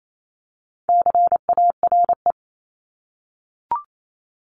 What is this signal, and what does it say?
Morse code, heard as a keyed tone, about 28 wpm, 700 Hz. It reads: CARE
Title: Multi-letter Phonemes 28wpm